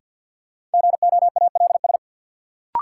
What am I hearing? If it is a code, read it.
Morse code, heard as a keyed tone, about 53 wpm, 700 Hz. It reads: GOALS
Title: Sentences from Top 1000 Words - Review Part 6 - New Word Drill - 50wpm